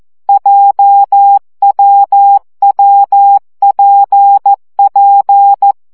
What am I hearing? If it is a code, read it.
Morse code, heard as a keyed tone, about 14 wpm, 800 Hz. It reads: JWWPP
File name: CW gyakorlas w p j 12 baud